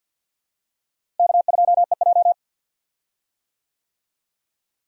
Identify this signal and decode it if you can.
Morse code, heard as a keyed tone, about 51 wpm, 700 Hz. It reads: K2EJ
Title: USA Call Signs - 4 Characters - Repeated 3x - 50wpm